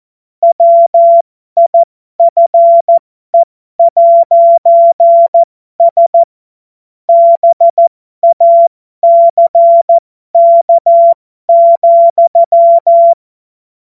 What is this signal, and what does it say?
Morse code, heard as a keyed tone, about 14 wpm, spaced as for 12 wpm, 680 Hz. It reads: WIFE'S BACK,